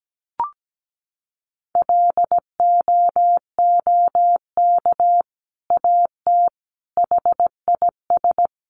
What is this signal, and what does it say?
Morse code, heard as a keyed tone, about 17 wpm, 700 Hz. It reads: LOOK AT HIS
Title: Sentences from Top 1500 Words 17wpm